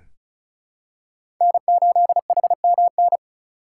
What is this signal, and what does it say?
Morse code, heard as a keyed tone, about 36 wpm, 700 Hz. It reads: N8HMN
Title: USA Call Signs - 5 Characters - 4x Character Spacing - 35wpm